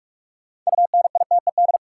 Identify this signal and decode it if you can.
Morse code, heard as a keyed tone, about 46 wpm, 700 Hz. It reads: UNITED
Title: Sentences from Top 600 Words - New Word Drill - 45wpm